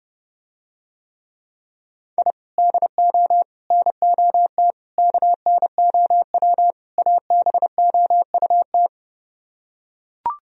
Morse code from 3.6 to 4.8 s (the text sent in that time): NOT